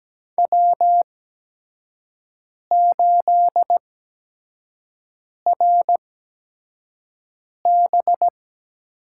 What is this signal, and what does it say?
Morse code, heard as a keyed tone, about 17 wpm, spaced as for 5 wpm, 700 Hz. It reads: W8RB
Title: USA Call Signs - 5 Characters - 8x Character Spacing - 17wpm